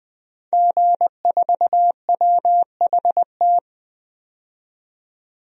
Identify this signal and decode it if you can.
Morse code, heard as a keyed tone, about 20 wpm, 700 Hz. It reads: G4WHT
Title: DXCC-100 Call Signs - 5 Characters - 20wpm